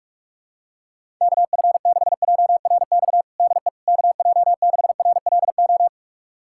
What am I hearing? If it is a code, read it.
Morse code, heard as a keyed tone, about 46 wpm, 690 Hz. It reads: KF6JRX DE KJ6RLO
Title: USA Callsign DE Callsign 45wpm